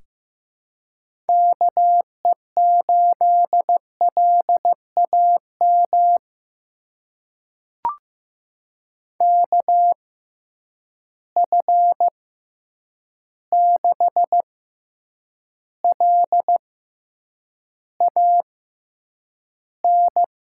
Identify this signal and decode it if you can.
Morse code, heard as a keyed tone, about 15 wpm, spaced as for 8 wpm, 700 Hz. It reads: KE8LAM KF6LAN
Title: USA Call Signs - 6 Characters - 6x Character Spacing - 15wpm